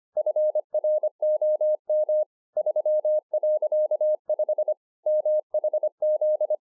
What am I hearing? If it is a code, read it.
Morse code, heard as a keyed tone, about 24 wpm, 610 Hz. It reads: FROM 3.5 MHZ